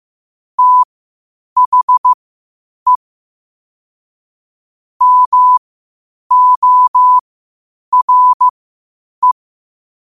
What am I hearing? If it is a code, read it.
Morse code, heard as a keyed tone, about 15 wpm, spaced as for 8 wpm, 1000 Hz. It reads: THE MORE
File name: qotdfiles / 8wpm